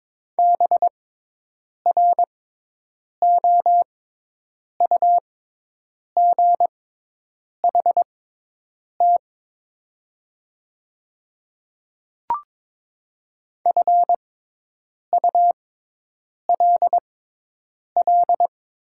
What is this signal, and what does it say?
Morse code, heard as a keyed tone, about 22 wpm, spaced as for 8 wpm, 700 Hz. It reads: BROUGHT FULL
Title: 6x Character Spacing - Top 500 Words - 22wpm